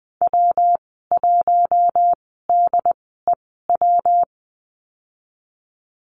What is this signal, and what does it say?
Morse code, heard as a keyed tone, about 20 wpm, spaced as for 15 wpm, 700 Hz. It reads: W1DEW
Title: 173 Lesson 28g - Number 9 - Call Signs 20wpm